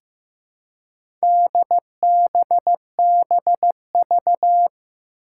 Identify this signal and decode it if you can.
Morse code, heard as a keyed tone, about 15 wpm, 700 Hz. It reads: DBBV
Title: Four Challenging Letter-Number Combinations 15wpm